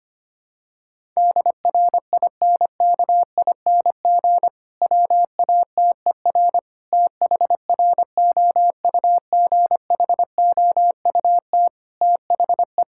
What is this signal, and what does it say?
Morse code, heard as a keyed tone, about 25 wpm, 700 Hz. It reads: DRINKING WATER THROUGHOUT THE
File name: practice / 25wpm / Difficult Sentences with Hints - 25wpm